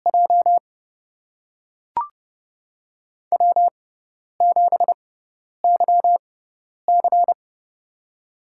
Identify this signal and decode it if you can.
Morse code, heard as a keyed tone, about 30 wpm, spaced as for 10 wpm, 700 Hz. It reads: J W7YC
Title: USA Call Signs - 4 Characters - 6x Character Spacing - 30wpm